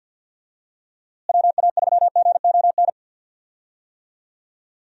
Morse code, heard as a keyed tone, about 47 wpm, 700 Hz. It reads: WA3GON
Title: USA Call Signs - 6 Characters - Repeated 2x - 50wpm